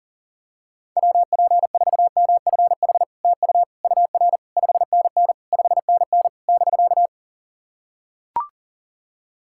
Morse code, heard as a keyed tone, about 40 wpm, 700 Hz. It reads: WP4MFH TU UR 5NN 5NN <BK>
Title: SOTA QSO Practice 40wpm